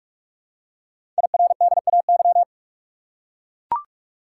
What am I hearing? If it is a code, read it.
Morse code, heard as a keyed tone, about 44 wpm, 700 Hz. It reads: IRDAY
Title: Code Groups - 5 Letters - 45wpm